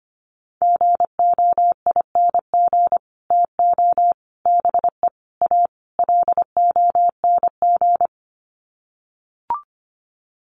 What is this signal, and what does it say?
Morse code, heard as a keyed tone, about 25 wpm, 700 Hz. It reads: GOING TO BE A LONG
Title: Sets of 5 Words Encore2 25wpm